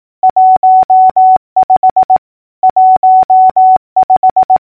Morse code, heard as a keyed tone, about 18 wpm, 750 Hz. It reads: 15 15
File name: numbers / V2-morse-clicks